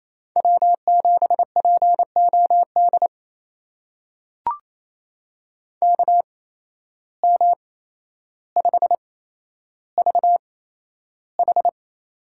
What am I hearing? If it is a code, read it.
Morse code, heard as a keyed tone, about 28 wpm, spaced as for 12 wpm, 700 Hz. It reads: W7POD KM5VH
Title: USA Call Signs - 5 Characters - 8x Character Spacing - 28wpm